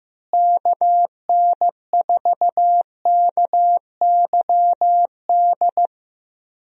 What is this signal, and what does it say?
Morse code, heard as a keyed tone, about 15 wpm, 700 Hz. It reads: KN4KYD
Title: USA Call Signs - 6 Characters - Repeated 2x - 15wpm